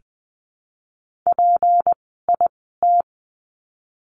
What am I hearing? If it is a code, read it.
Morse code, heard as a keyed tone, about 20 wpm, spaced as for 15 wpm, 700 Hz. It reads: PIT
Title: 153 Lesson 25f - Letter p - Words 20wpm